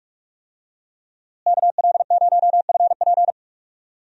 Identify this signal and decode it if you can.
Morse code, heard as a keyed tone, about 47 wpm, 700 Hz. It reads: KL0FP
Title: USA Call Signs - 5 Characters - Repeated 2x - 45wpm